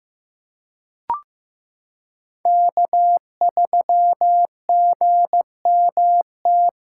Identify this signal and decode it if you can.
Morse code, heard as a keyed tone, about 15 wpm, 700 Hz. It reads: K3GMT
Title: K1USN SST QSO Practice - 15wpm